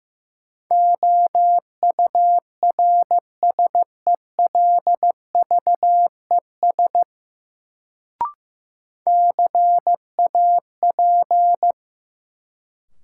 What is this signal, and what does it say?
Morse code, heard as a keyed tone, about 15 wpm, 700 Hz. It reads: OURSELVES CAP
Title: Sentences from Top 1600 Words - New Word Drill - 15wpm